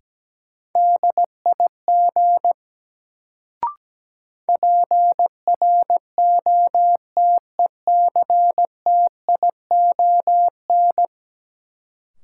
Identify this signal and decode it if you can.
Morse code, heard as a keyed tone, about 17 wpm, 700 Hz. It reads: DIG PROTECTION
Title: Sentences from Top 1600 Words - New Word Drill Alt - 17wpm